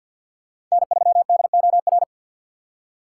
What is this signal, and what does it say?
Morse code, heard as a keyed tone, about 51 wpm, 700 Hz. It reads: N3DOR
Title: USA Call Signs - 5 Characters - 3x Character Spacing - 50wpm